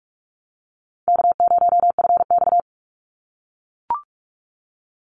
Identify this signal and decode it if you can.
Morse code, heard as a keyed tone, about 44 wpm, 700 Hz. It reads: K0FX
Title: Easy Call Signs - 45wpm